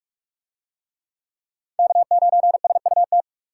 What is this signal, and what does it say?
Morse code, heard as a keyed tone, about 43 wpm, 700 Hz. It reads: K9SUT